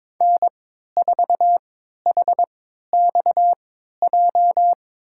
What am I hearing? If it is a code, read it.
Morse code, heard as a keyed tone, about 22 wpm, spaced as for 13 wpm, 700 Hz. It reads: N4HXJ
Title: USA Call Signs - 5 Characters - 3x Character Spacing - 22wpm